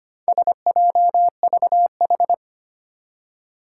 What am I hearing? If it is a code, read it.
Morse code, heard as a keyed tone, about 25 wpm, 700 Hz. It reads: SJVH
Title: Four Challenging Letter-Number Combinations 25wpm